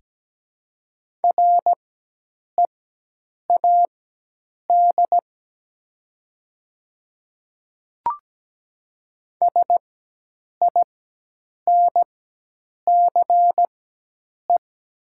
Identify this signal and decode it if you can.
Morse code, heard as a keyed tone, about 17 wpm, spaced as for 8 wpm, 700 Hz. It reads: READ SINCE
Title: 4x Character Spacing - Top 300 Words - 17wpm